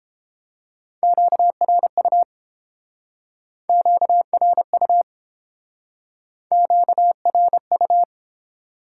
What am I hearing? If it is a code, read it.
Morse code, heard as a keyed tone, about 30 wpm, 700 Hz. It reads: QRU QRU QRU